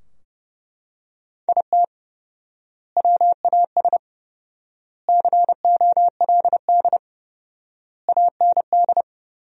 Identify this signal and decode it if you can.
Morse code, heard as a keyed tone, about 29 wpm, 700 Hz. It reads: IT WAS COLD AND
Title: 4x Word-Spacing - Sentences from Top 500 Words - 30wpm